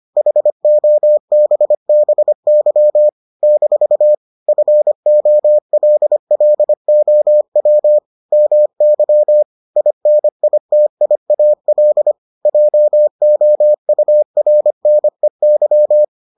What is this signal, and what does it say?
Morse code, heard as a keyed tone, about 25 wpm, 600 Hz. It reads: HOBBY - FOLLOW MY INITIAL JOURNEY